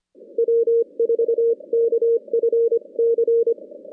Morse code, heard as a keyed tone, about 25 wpm, 460 Hz. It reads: W4KFC
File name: contests / CW-SS-k5tr- -2100